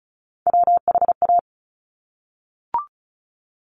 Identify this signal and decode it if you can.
Morse code, heard as a keyed tone, about 36 wpm, 700 Hz. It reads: WHA
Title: Top Three Letter Combinations in Words 35wpm